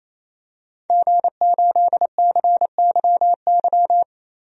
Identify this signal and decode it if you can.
Morse code, heard as a keyed tone, about 28 wpm, 700 Hz. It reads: G8CYY
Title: DXCC-100 Call Signs - 5 Characters - 28wpm